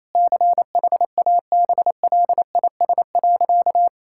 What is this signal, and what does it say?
Morse code, heard as a keyed tone, about 27 wpm, 700 Hz. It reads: CHABLIS.
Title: Pangram Challenge v2 - 28wpm